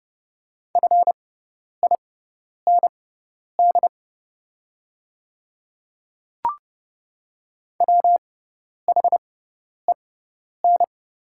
Morse code, heard as a keyed tone, about 31 wpm, spaced as for 10 wpm, 700 Hz. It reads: FIND WHEN
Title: 6x Character Spacing - Top 300 Words - 30wpm